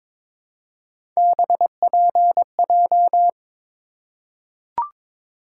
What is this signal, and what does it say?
Morse code, heard as a keyed tone, about 22 wpm, 700 Hz. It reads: BPJ
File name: practice / 22wpm / Three Letter-Number Combinations - 22wpm